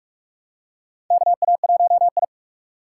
Morse code, heard as a keyed tone, about 46 wpm, 700 Hz. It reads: KA1I
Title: USA Call Signs - 4 Characters - 2x Character Spacing - 45wpm